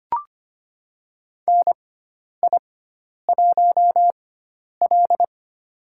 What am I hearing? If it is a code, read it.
Morse code, heard as a keyed tone, about 25 wpm, spaced as for 10 wpm, 700 Hz. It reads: NI1L
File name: Lesson 16g - Letter c - Call Signs 25wpm